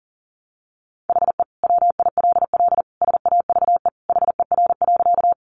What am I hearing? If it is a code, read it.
Morse code, heard as a keyed tone, about 40 wpm, 700 Hz. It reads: HE WILL SAVE HER.
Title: Sentences from Top 1100 Words 40wpm